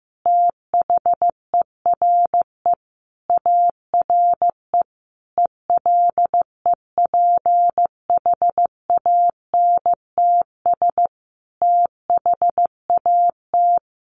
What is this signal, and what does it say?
Morse code, heard as a keyed tone, about 15 wpm, 700 Hz. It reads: THERE ARE ELEPHANTS THAT
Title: Easy Phrases 15wpm